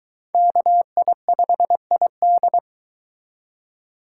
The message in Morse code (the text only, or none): KI5ID